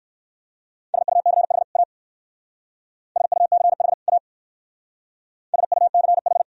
Morse code, heard as a keyed tone, about 57 wpm, 690 Hz. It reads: SUCH A SUCH A SUCH